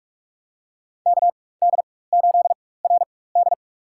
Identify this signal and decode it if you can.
Morse code, heard as a keyed tone, about 47 wpm, spaced as for 21 wpm, 700 Hz. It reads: KD8RD
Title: USA Call Signs - 4x Character Spacing - 45wpm